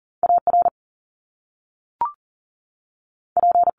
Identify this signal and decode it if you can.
Morse code, heard as a keyed tone, about 41 wpm, spaced as for 6 wpm, 700 Hz. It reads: ARP
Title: Top Two Letter Combinations in Words 40wpm